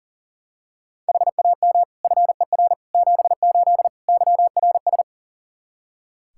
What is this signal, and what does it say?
Morse code, heard as a keyed tone, about 40 wpm, 700 Hz. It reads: HAM FER 78 YRS
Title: Practice QSO - Exchange 3 - More Info - 40wpm